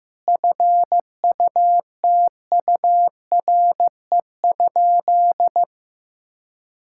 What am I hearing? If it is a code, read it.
Morse code, heard as a keyed tone, about 15 wpm, 700 Hz. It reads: FUTURE?
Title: Sentences from Top 1000 Words - Review Part 6 - 15wpm